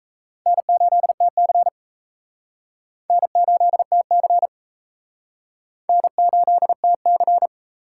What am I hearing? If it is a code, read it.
Morse code, heard as a keyed tone, about 37 wpm, 700 Hz. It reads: N8TC N8TC N8TC